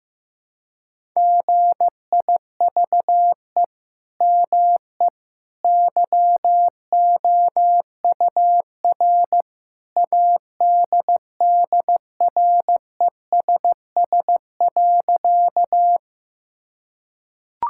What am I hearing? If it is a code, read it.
Morse code, heard as a keyed tone, about 16 wpm, 700 Hz. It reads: GIVE ME YOUR ADDRESS.
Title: Sentences from Top 1000 Words - Review Part 6 - 15wpm